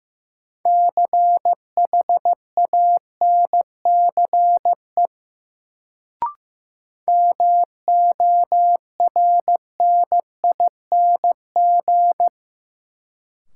Morse code, heard as a keyed tone, about 15 wpm, 700 Hz. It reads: CHANCE MORNING
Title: Sets of 3 Words - New Word Drill - 15wpm